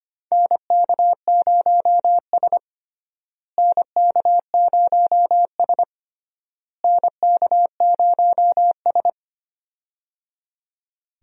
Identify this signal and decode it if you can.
Morse code, heard as a keyed tone, about 25 wpm, 700 Hz. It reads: NK0S NK0S NK0S
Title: USA Call Signs - 4 Characters - Repeated 3x - 25wpm